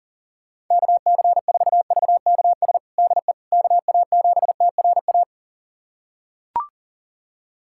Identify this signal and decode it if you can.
Morse code, heard as a keyed tone, about 40 wpm, 700 Hz. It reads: KC4VKS DE KA7TRA
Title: USA Callsign DE Callsign 40wpm